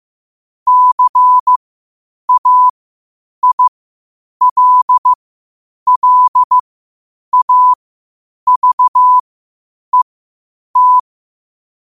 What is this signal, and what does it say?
Morse code, heard as a keyed tone, about 15 wpm, spaced as for 8 wpm, 1000 Hz. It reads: CAILLAVET
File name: qotdfiles / 8wpm